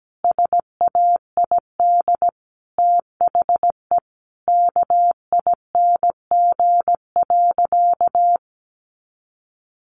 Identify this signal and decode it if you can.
Morse code, heard as a keyed tone, about 17 wpm, 700 Hz. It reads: SAID THE KING.
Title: Sentences from Top 400 Words 17wpm